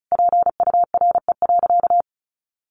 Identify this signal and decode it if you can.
Morse code, heard as a keyed tone, about 34 wpm, 700 Hz. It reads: PURE.
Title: Sentences from Top 1500 Words 35wpm